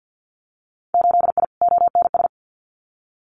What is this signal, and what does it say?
Morse code, heard as a keyed tone, about 50 wpm, 700 Hz. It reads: 8I ONS
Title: Winter Field Day Exchanges Encore 50wpm